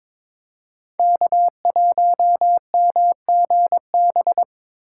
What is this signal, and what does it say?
Morse code, heard as a keyed tone, about 24 wpm, 700 Hz. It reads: K1MGB
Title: Speed Racing - USA Call Signs 22wpm